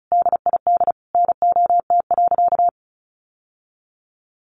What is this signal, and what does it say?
Morse code, heard as a keyed tone, about 34 wpm, 700 Hz. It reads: DID NOT.
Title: Sentences from Top 700 Words 35wpm